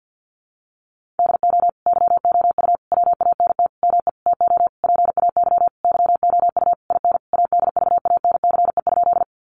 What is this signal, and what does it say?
Morse code, heard as a keyed tone, about 50 wpm, 700 Hz. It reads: DO YOU WANT ME TO PAY YOU IN ADVANCE?
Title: Sentences from Top 1400 Words 50wpm